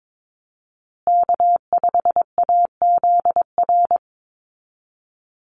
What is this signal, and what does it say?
Morse code, heard as a keyed tone, about 22 wpm, 700 Hz. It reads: K5AZR
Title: Call Signs Repeated 3x - 22wpm